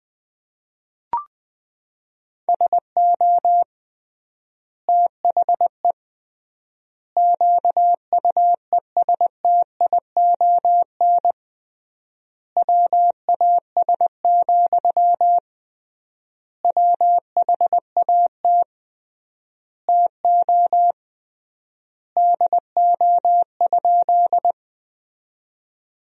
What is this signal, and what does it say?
Morse code, heard as a keyed tone, about 20 wpm, 700 Hz. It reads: SO THE QUESTION WAS, WHAT TO DO?
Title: 3x Word-Spacing - Sentences from Top 400 Words - 20wpm